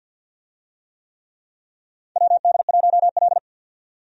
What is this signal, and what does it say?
Morse code, heard as a keyed tone, about 50 wpm, 700 Hz. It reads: WD1L
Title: USA Call Signs - 4 Characters - 50wpm